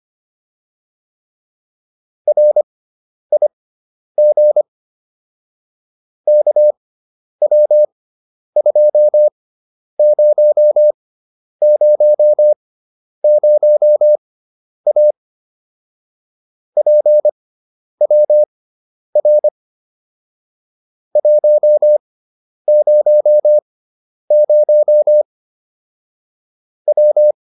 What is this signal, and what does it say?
Morse code, heard as a keyed tone, about 25 wpm, spaced as for 10 wpm, 600 Hz. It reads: RIG KW2000A PWR 100 W